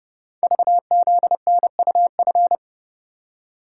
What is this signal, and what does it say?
Morse code, heard as a keyed tone, about 29 wpm, 700 Hz. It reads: VZNUF
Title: Code Groups - 5 Letter:Numbers - 30wpm